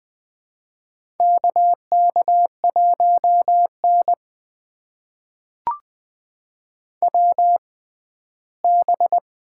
Morse code, heard as a keyed tone, about 20 wpm, spaced as for 13 wpm, 700 Hz. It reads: KK1N WB
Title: USA Call Signs - 6x Character Spacing - 20wpm